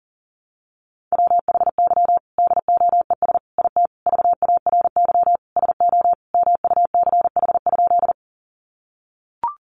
Morse code, heard as a keyed tone, about 40 wpm, 700 Hz. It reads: WHY DOES IT VARY SO MUCH?
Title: Sentences from Top 1200 Words 40wpm